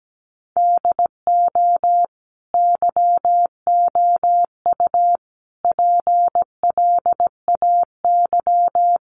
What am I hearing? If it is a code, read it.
Morse code, heard as a keyed tone, about 17 wpm, 700 Hz. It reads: DO YOU PLAY
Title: Sentences from Top 1500 Words 17wpm